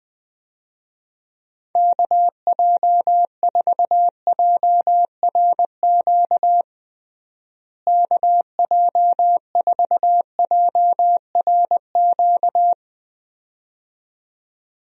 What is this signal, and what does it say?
Morse code, heard as a keyed tone, about 20 wpm, 700 Hz. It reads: KJ4JRQ KJ4JRQ
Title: USA Call Signs - 6 Characters - Repeated 2x - 20wpm